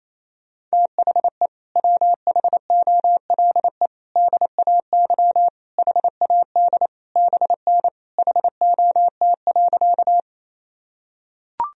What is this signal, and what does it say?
Morse code, heard as a keyed tone, about 28 wpm, 700 Hz. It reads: THE WHOLE DAY HAD BN HOT.
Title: Sentences from Top 500 Words and Abbreviations 28wpm